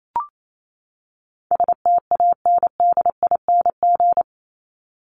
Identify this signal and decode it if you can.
Morse code, heard as a keyed tone, about 28 wpm, 700 Hz. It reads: STANDING
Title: ING Suffix - Top 100 Words with Base Words 28wpm